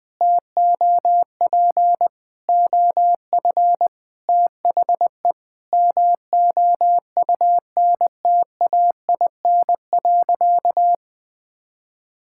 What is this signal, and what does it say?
Morse code, heard as a keyed tone, about 20 wpm, 700 Hz. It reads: TOP OF THE MOUNTAIN.
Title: Sentences with 3-Word Phrases - 20wpm